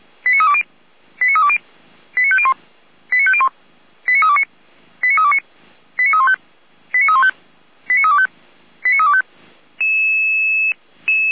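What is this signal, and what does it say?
Five-tone selective call (ZVEI), heard as quick runs of five tones, tones of about 70 ms. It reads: 89320 89320 89501 89501 89329 89329 89315 89315 89325 89325
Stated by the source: ZVEI five-tone selective call 89504 received on SDR.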